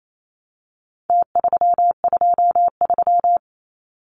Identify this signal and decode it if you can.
Morse code, heard as a keyed tone, about 28 wpm, 700 Hz. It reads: T323
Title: Numbers - Contest Style - 4-Digit 28wpm